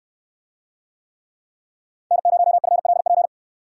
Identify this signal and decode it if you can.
Morse code, heard as a keyed tone, about 66 wpm, 700 Hz. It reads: N0FLP